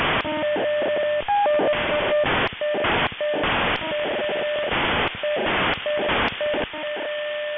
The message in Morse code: none

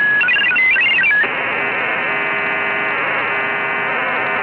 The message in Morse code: none

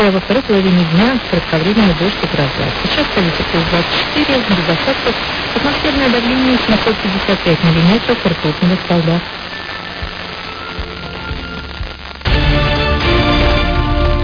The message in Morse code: none

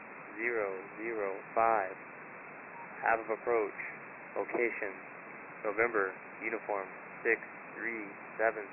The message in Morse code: none

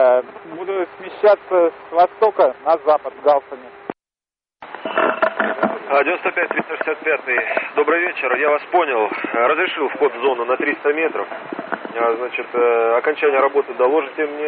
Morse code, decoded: none

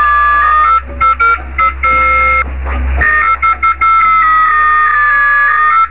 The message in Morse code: none

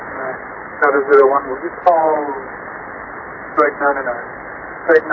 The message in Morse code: none

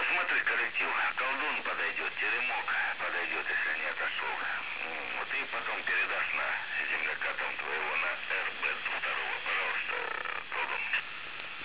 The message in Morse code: none

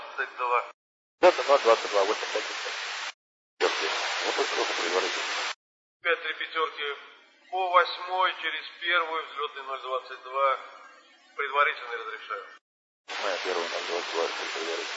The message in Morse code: none

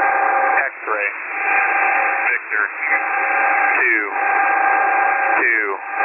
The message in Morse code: none

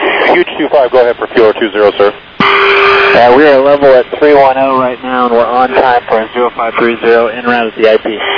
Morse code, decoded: none

